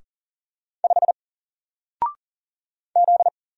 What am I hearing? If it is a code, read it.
Morse code, heard as a keyed tone, about 40 wpm, 700 Hz. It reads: 5 Z
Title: Single Letter-Number Alt 40wpm